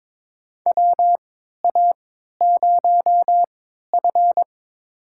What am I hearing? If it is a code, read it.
Morse code, heard as a keyed tone, about 22 wpm, spaced as for 13 wpm, 700 Hz. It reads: WA0F
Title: USA Call Signs - 4 Characters - 3x Character Spacing - 22wpm